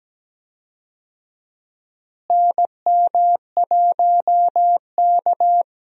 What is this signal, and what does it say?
Morse code, heard as a keyed tone, about 17 wpm, 700 Hz. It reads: NM1K
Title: USA Call Signs - 4 Characters - Repeated 3x - 17wpm